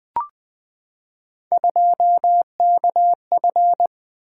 Morse code, heard as a keyed tone, about 20 wpm, 700 Hz. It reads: 2KF
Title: Three Letter-Number Combinations - 20wpm